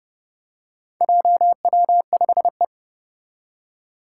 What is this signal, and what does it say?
Morse code, heard as a keyed tone, about 30 wpm, 700 Hz. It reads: JW5E
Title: Four Challenging Letter-Number Combinations 30wpm